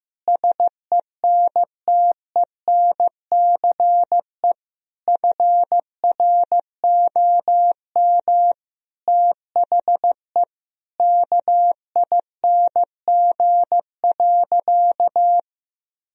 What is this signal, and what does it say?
Morse code, heard as a keyed tone, about 15 wpm, 700 Hz. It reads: SENTENCE FROM THE KING.
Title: Sentences from Top 1000 Words - Review Part 1 - 15wpm